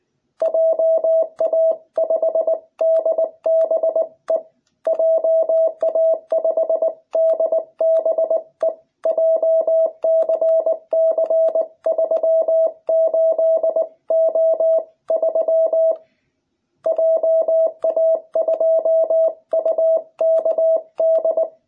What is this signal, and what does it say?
Morse code, heard as a keyed tone, about 20 wpm, 660 Hz. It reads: JA5DBE JA5DBE JCC38O3 JA2UKD